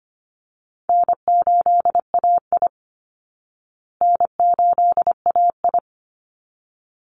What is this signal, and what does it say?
Morse code, heard as a keyed tone, about 25 wpm, 700 Hz. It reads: N8AI N8AI
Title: Call Signs Repeated 3x - 25wpm